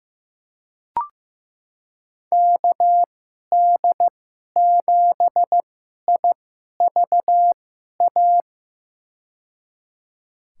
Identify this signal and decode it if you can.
Morse code, heard as a keyed tone, about 15 wpm, spaced as for 11 wpm, 700 Hz. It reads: KD7IVA
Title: USA Call Signs - 2x Character Spacing - 15wpm